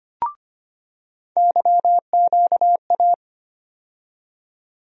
Morse code, heard as a keyed tone, about 25 wpm, 700 Hz. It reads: YQA
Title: USA Call Sign Postfixes - 3 Letters 25wpm